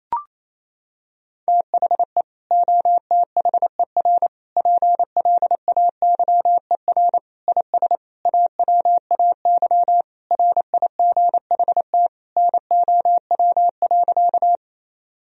Morse code, heard as a keyed tone, about 27 wpm, 700 Hz. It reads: THE OTHER PLAYER IS AWAY RIGHT NOW.
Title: Sentences from Top 1000 Words - Review Part 4 - 28wpm